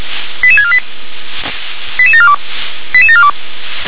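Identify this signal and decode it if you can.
Five-tone selective call (ZVEI), heard as quick runs of five tones, tones of about 70 ms. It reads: 88648 88742 88742
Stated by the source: Five-tone ZVEI selective call received by SDR.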